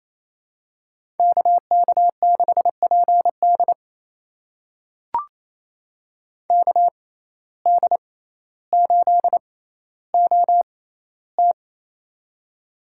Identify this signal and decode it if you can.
Morse code, heard as a keyed tone, about 28 wpm, spaced as for 14 wpm, 700 Hz. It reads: KK6PD KD8OT
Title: USA Call Signs - 5 Characters - 6x Character Spacing - 28wpm